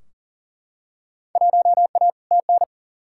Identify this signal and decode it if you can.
Morse code, heard as a keyed tone, about 40 wpm, 700 Hz. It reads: 1A TN
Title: Field Day Exchanges - 40wpm